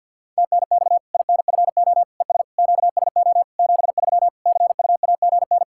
Morse code, heard as a keyed tone, about 53 wpm, 700 Hz. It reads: TNX INFO ES QSO 73 CUAGN
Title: Practice QSO - Exchange 4 - Ending - 50wpm